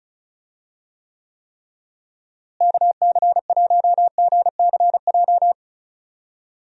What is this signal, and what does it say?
Morse code, heard as a keyed tone, about 36 wpm, 700 Hz. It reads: KC1GCJ
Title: USA Call Signs - 6 Characters - Repeated 3x - 35wpm